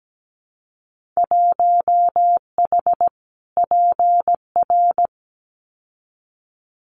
Morse code, heard as a keyed tone, about 17 wpm, 700 Hz. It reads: 1H PR
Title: Winter Field Day Exchanges Encore 17wpm